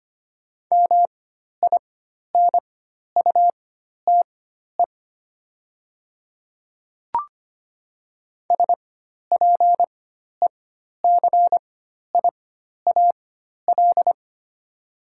Morse code, heard as a keyed tone, about 25 wpm, spaced as for 12 wpm, 700 Hz. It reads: MINUTE SPECIAL
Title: 4x Character Spacing - Top 500 Words - 25wpm